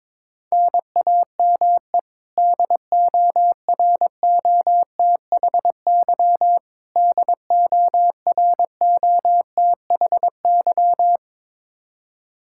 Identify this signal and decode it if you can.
Morse code, heard as a keyed tone, about 22 wpm, 700 Hz. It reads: NAME DOROTHY DOROTHY
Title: Practice QSO - Exchange 1 - Beginning - 22wpm